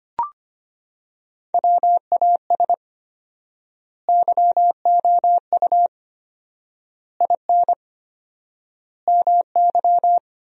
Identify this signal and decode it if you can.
Morse code, heard as a keyed tone, about 25 wpm, 700 Hz. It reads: WAS YOU IN MY
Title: 4x Word-Spacing - Sentences from Top 300 Words - 25wpm